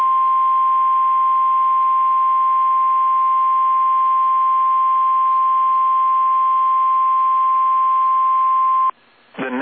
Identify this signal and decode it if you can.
Emergency Alert System (SAME) tones (the 1050 Hz attention tone), with speech. It is heard from 1.2 to 8.6 s.
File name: eas / KSDZKDJL / KSDZKDJL